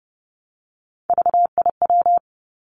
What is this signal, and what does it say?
Morse code, heard as a keyed tone, about 30 wpm, 700 Hz. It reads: VIW
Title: Three Challenging Letter-Number Combinations 30wpm